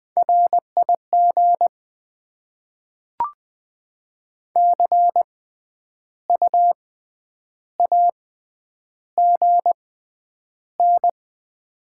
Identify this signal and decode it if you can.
Morse code, heard as a keyed tone, about 20 wpm, spaced as for 9 wpm, 700 Hz. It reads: RIG CUAGN
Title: Top 20 QSO Elements - 6x Character Spacing - 20wpm